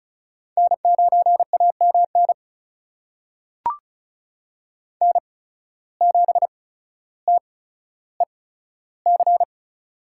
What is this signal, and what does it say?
Morse code, heard as a keyed tone, about 36 wpm, spaced as for 15 wpm, 700 Hz. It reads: N9AMN N7TEC
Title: USA Call Signs - 5 Characters - 8x Character Spacing - 35wpm